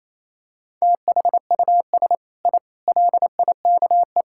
Morse code, heard as a keyed tone, about 28 wpm, 700 Hz. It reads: THUS I LIKE
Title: Sentences from Top 1000 Words - Review Part 5 - 28wpm